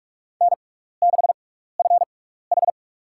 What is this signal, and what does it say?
Morse code, heard as a keyed tone, about 46 wpm, spaced as for 16 wpm, 700 Hz. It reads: N6FH
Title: USA Call Signs - 4 Characters - 6x Character Spacing - 45wpm